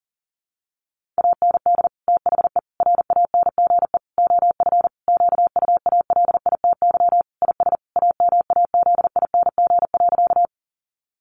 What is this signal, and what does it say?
Morse code, heard as a keyed tone, about 40 wpm, 700 Hz. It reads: AND THE RANGE OF QUALITY IS AMAZING.
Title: Sentences from Top 1500 Words 40wpm